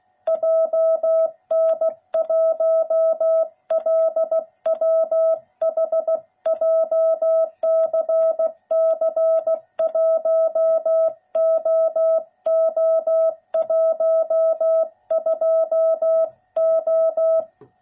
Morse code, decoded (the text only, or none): JN1LWH JCC 1OO12O